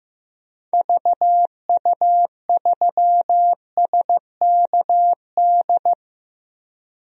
Morse code, heard as a keyed tone, about 16 wpm, 700 Hz. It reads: VU3SKD
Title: DXCC-100 Call Signs - 6 Characters - Repeated 2x - 15wpm